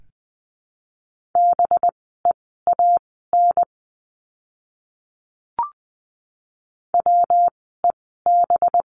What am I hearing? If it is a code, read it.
Morse code, heard as a keyed tone, about 20 wpm, spaced as for 14 wpm, 700 Hz. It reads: BEAN WEB
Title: 185 Lesson 30f - Letter b - Words 20wpm